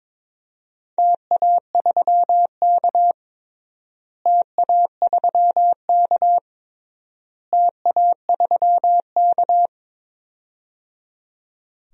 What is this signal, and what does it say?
Morse code, heard as a keyed tone, about 22 wpm, 700 Hz. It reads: TA3K TA3K TA3K
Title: DXCC-100 Call Signs - 4 Characters - Repeated 3x - 22wpm